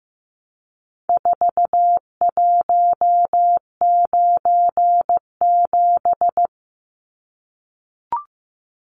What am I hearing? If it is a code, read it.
Morse code, heard as a keyed tone, about 15 wpm, 700 Hz. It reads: 4197
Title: Numbers - 4-Digit 15wpm